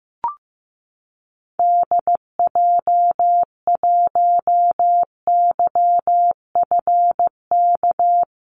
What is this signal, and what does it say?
Morse code, heard as a keyed tone, about 15 wpm, 700 Hz. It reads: DJ1YFK
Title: Call Signs Repeated 2x - 15wpm